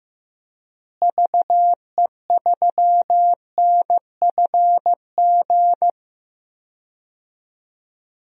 Morse code, heard as a keyed tone, about 15 wpm, 700 Hz. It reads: VE3NFG
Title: DXCC-100 Call Signs - 6 Characters - Repeated 3x - 15wpm